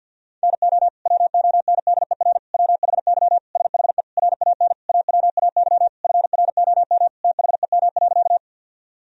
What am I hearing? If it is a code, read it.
Morse code, heard as a keyed tone, about 50 wpm, 700 Hz. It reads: NO WONDER WHY SHE RAN AWAY FROM THEM.
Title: Sentences from Top 1000 Words - Review Part 3 - 50wpm